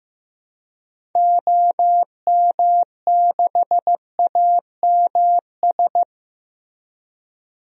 Morse code, heard as a keyed tone, about 15 wpm, 700 Hz. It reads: OM6AMS
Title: DXCC-100 Call Signs - 6 Characters - Repeated 2x - 15wpm